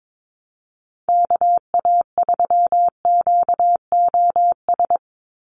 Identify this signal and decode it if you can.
Morse code, heard as a keyed tone, about 22 wpm, 700 Hz. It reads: KA3QOS
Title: USA Call Signs 22wpm